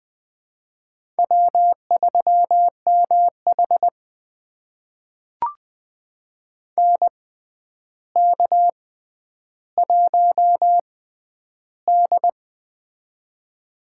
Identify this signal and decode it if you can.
Morse code, heard as a keyed tone, about 20 wpm, spaced as for 10 wpm, 700 Hz. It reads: W3MH NK1D
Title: USA Call Signs - 4 Characters - 6x Character Spacing - 20wpm